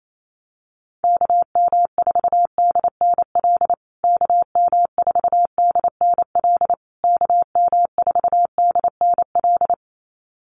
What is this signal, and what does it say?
Morse code, heard as a keyed tone, about 28 wpm, 700 Hz. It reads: KM4DNL KM4DNL KM4DNL
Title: USA Call Signs Repeated 3x 28wpm